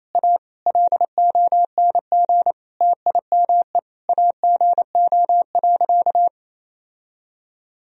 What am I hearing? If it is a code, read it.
Morse code, heard as a keyed tone, about 28 wpm, 700 Hz. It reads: A LONG TIME AGO.
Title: Sentences from Top 1000 Words - Review Part 3 - 28wpm